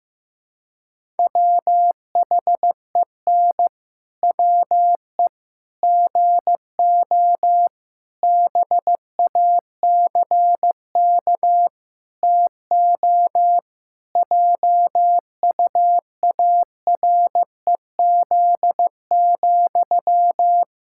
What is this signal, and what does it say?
Morse code, heard as a keyed tone, about 15 wpm, 700 Hz. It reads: WHEN WE GO BACK TO JUAREZ,
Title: Pangram Challenge v2 - 15wpm